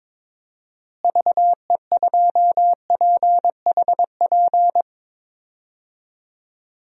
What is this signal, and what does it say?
Morse code, heard as a keyed tone, about 22 wpm, 700 Hz. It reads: VE2PHP
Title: DXCC-100 Call Signs - 6 Characters - Repeated 2x - 22wpm